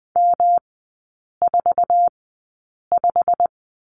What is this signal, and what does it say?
Morse code, heard as a keyed tone, about 20 wpm, spaced as for 8 wpm, 700 Hz. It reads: M45
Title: 107 Lesson 18e - Letter m - Three Characters 20wpm